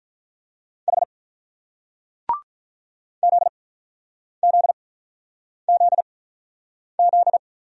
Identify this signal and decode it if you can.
Morse code, heard as a keyed tone, about 48 wpm, spaced as for 9 wpm, 700 Hz. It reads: H ZZZZ